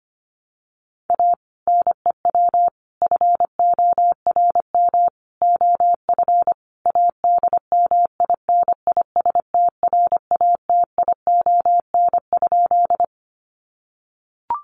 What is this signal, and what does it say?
Morse code, heard as a keyed tone, about 25 wpm, 700 Hz. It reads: A NEW FORM OF ADMINISTRATION?
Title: Sentences from Top 1300 Words 25wpm